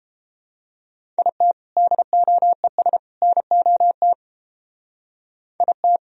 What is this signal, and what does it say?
Morse code, heard as a keyed tone, about 29 wpm, 700 Hz. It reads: IT DOES NOT IT